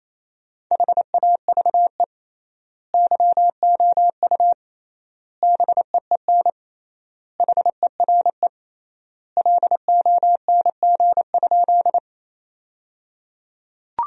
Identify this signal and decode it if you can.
Morse code, heard as a keyed tone, about 28 wpm, 700 Hz. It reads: HAVE YOU BEEN HERE LONG?
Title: 3x Word-Spacing - Sentences from Top 200 Words - 28wpm